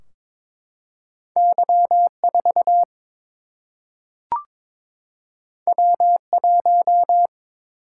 Morse code, heard as a keyed tone, about 22 wpm, 700 Hz. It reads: Y4 W1
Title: Single-Letter Single-Number 22wpm